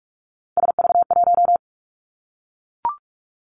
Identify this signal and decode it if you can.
Morse code, heard as a keyed tone, about 44 wpm, 700 Hz. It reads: SV1
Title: Callsign Prefixes for Top 50 DXCC - 45wpm